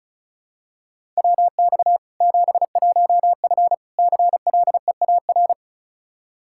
Morse code, heard as a keyed tone, about 36 wpm, 700 Hz. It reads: WX 71F CLEAR
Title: K1USN SST QSO Practice 35wpm